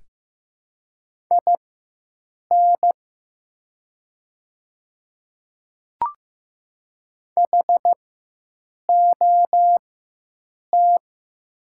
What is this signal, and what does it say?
Morse code, heard as a keyed tone, about 15 wpm, spaced as for 7 wpm, 700 Hz. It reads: IN HOT
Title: 4x Character Spacing - Top 100 Words - 15wpm